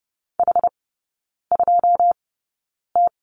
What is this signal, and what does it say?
Morse code, heard as a keyed tone, about 30 wpm, spaced as for 9 wpm, 700 Hz. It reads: H2T
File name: Lesson 17e - Letter u - Three Characters 30wpm